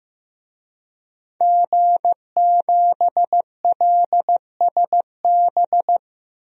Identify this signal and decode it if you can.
Morse code, heard as a keyed tone, about 15 wpm, 700 Hz. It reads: G7LSB
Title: DXCC-100 Call Signs - 5 Characters - Repeated 2x - 15wpm